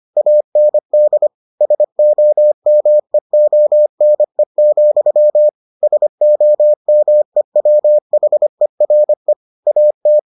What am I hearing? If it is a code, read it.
Morse code, heard as a keyed tone, about 25 wpm, 600 Hz. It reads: AND SOMEONE, SOMEWHERE AT